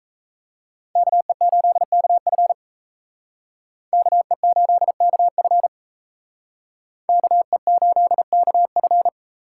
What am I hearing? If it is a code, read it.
Morse code, heard as a keyed tone, about 34 wpm, 700 Hz. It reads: KE8KF KE8KF KE8KF
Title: Speed Racing - USA 2x2 Call Signs 28wpm